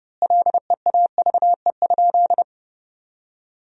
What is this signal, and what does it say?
Morse code, heard as a keyed tone, about 32 wpm, 700 Hz. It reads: LEAVE?
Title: 2x Word-Spacing - Sentences from Top 400 Words - 30wpm